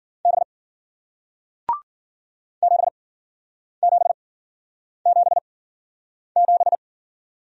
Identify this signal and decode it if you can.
Morse code, heard as a keyed tone, about 50 wpm, spaced as for 9 wpm, 700 Hz. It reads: B 7777
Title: Speed-Racing Single Letter-Number 40wpm